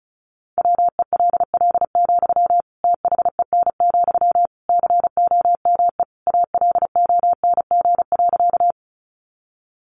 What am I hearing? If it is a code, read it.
Morse code, heard as a keyed tone, about 34 wpm, 700 Hz. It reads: WELL, THEN, COME ALONG.
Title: Sentences from Top 700 Words 35wpm